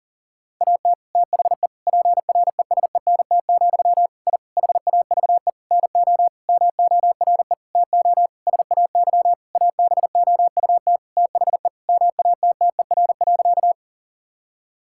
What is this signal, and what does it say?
Morse code, heard as a keyed tone, about 40 wpm, 700 Hz. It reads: AT THE PRESENT, I HAVE NO MORE TO SAY ABOUT THE MATTER.
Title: Sentences from Top 1000 Words - Review Part 5 - 40wpm